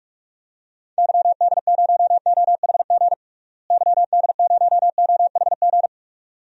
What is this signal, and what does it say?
Morse code, heard as a keyed tone, about 44 wpm, 700 Hz. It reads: YD0OHG YD0OHG
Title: DXCC-100 Call Signs - 6 Characters - Repeated 2x - 45wpm